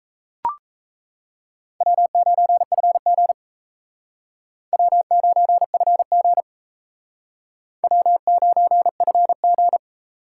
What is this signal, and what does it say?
Morse code, heard as a keyed tone, about 36 wpm, 700 Hz. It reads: W9FG W9FG W9FG